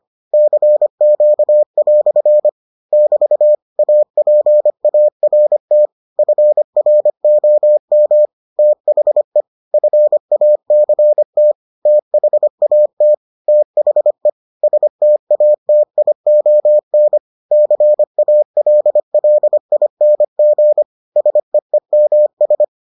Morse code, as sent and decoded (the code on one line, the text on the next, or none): -.-. --.- .-..-. / -...- / .- .--. .- .-. - / ..-. .-. --- -- / - .... / ..-. .- -.-. - / - .... .- - / - .... / ... - .- - .. --- -. / -.-. .- .-.. .-.. .. -. --. / ... . -- ...
CQ" <BT> APART FROM THE FACT THAT THE STATION CALLING SEEMS